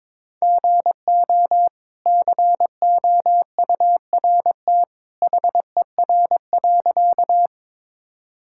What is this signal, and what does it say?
Morse code, heard as a keyed tone, about 22 wpm, 700 Hz. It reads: GO COURT HER.
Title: Sentences from Top 1600 Words 22wpm